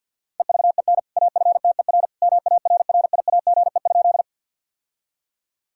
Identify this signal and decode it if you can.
Morse code, heard as a keyed tone, about 50 wpm, 700 Hz. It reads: EVEN AFTER MARRIAGE?
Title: Sentences from Top 1000 Words - Review Part 5 - 50wpm